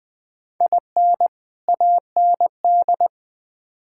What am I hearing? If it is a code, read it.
Morse code, heard as a keyed tone, about 20 wpm, 700 Hz. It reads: IN AND
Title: Build A Phrase - 4-Word Phrases - 20wpm